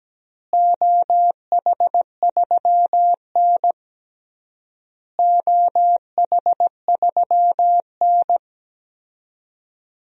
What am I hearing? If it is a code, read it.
Morse code, heard as a keyed tone, about 17 wpm, 700 Hz. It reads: OH3N OH3N
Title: DXCC-100 Call Signs - 4 Characters - Repeated 3x - 17wpm